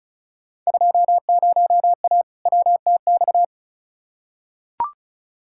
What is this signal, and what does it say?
Morse code, heard as a keyed tone, about 36 wpm, 700 Hz. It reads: 20A WTX
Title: Field Day Exchanges Encore - 35wpm